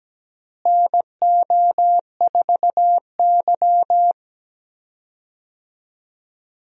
Morse code, heard as a keyed tone, about 17 wpm, 700 Hz. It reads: NO4Y
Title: USA Call Signs - 4 Characters - Repeated 2x - 17wpm